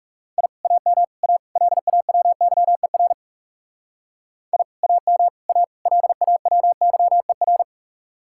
Morse code, heard as a keyed tone, about 37 wpm, 700 Hz. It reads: I AM A LAWYER I AM A LAWYER
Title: Speed-Racing Common QSO Phrases - 30wpm